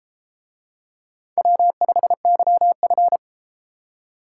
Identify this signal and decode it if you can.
Morse code, heard as a keyed tone, about 33 wpm, 700 Hz. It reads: W5YF